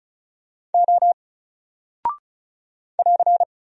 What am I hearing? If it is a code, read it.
Morse code, heard as a keyed tone, about 36 wpm, 700 Hz. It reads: O <AR>
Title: Single Character (Letter:Number:Punctuation:Prosign) 35wpm